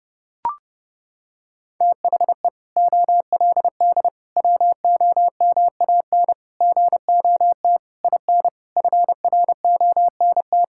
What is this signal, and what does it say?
Morse code, heard as a keyed tone, about 30 wpm, 700 Hz. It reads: THE OLD WOMAN GOT IN FRONT
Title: Sentences from Top 1600 Words 30wpm